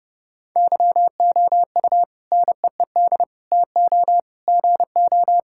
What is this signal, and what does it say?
Morse code, heard as a keyed tone, about 30 wpm, 700 Hz. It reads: YOU NEED TO GO
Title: Sentences from Top 1000 Words - Review Part 6 - 30wpm